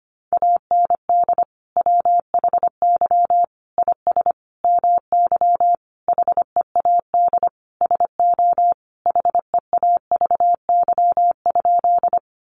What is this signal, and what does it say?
Morse code, heard as a keyed tone, about 25 wpm, 700 Hz. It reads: AND WHY IS MY HEAD SO HEAVY?
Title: Sentences from Top 1100 Words 25wpm